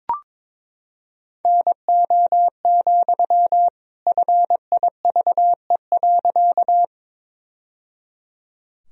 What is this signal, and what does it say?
Morse code, heard as a keyed tone, about 22 wpm, 700 Hz. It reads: NO, FIVE.
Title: Sentences from Top 1000 Words - Review Part 2 - 22wpm